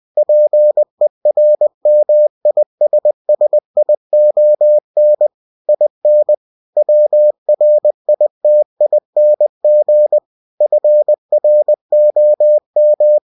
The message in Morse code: PERMISSION IN WRITING FROM

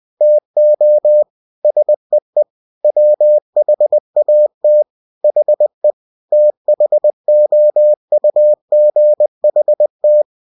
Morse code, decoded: TO SEE WHAT HE THOUGHT

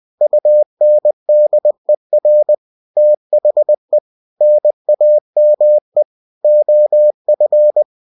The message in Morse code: UNDER THE NAME OF

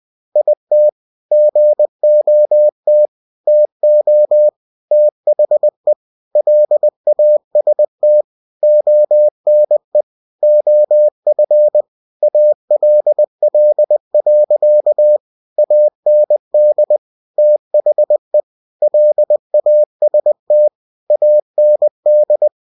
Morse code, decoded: IT GOT TO THE LAST ONE OF ALL. AND THE LAST AND

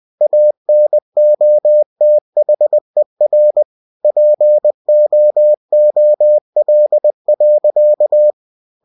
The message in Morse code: ANOTHER POOL.

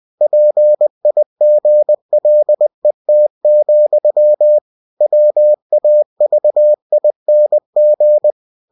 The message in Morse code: PIGLET, WAVING

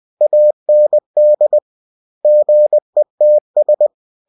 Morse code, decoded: AND GETS

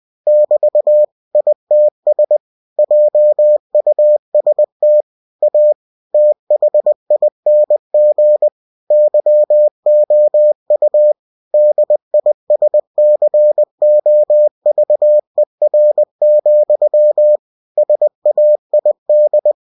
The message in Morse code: <BT> ITS JUST A THING YOU DISCOVER, SAID

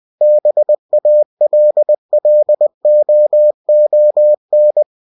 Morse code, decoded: BALLOON

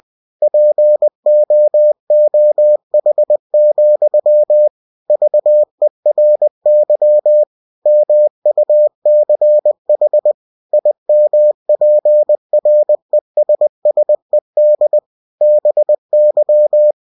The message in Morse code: POOH, VERY MUCH IMPRESSED BY

